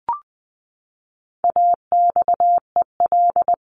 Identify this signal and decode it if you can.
Morse code, heard as a keyed tone, about 20 wpm, 700 Hz. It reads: AXEL
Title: CWT Exchange 20wpm